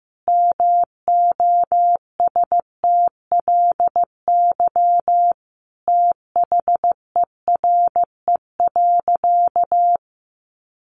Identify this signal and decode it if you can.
Morse code, heard as a keyed tone, about 15 wpm, 700 Hz. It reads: MOSTLY THERE.
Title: Sentences from Top 1300 Words 15wpm